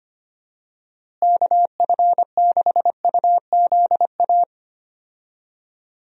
Morse code, heard as a keyed tone, about 25 wpm, 700 Hz. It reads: KF6UZA
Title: USA Call Signs - 6 Characters - Repeated 3x - 25wpm